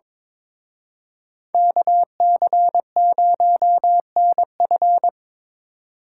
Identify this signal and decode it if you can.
Morse code, heard as a keyed tone, about 26 wpm, 700 Hz. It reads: KC0NF
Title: Speed Racing - USA 2x2 Call Signs 22wpm